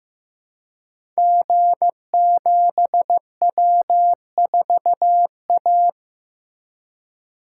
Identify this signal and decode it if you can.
Morse code, heard as a keyed tone, about 15 wpm, 700 Hz. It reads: G7W4A
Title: Code Groups - 5 Letter:Numbers - 15wpm